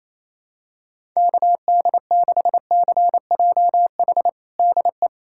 Morse code, heard as a keyed tone, about 28 wpm, 700 Hz. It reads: KD6CJH DE
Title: USA Callsign DE Callsign 28wpm